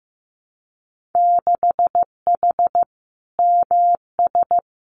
Morse code, heard as a keyed tone, about 15 wpm, 700 Hz. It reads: 6H MS
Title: Winter Field Day Exchanges Encore 15wpm